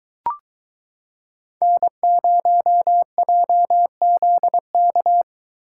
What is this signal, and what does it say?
Morse code, heard as a keyed tone, about 22 wpm, 700 Hz. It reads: N0JZK